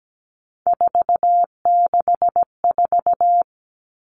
Morse code, heard as a keyed tone, about 17 wpm, 700 Hz. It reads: 464
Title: Numbers - 3-Digit 17wpm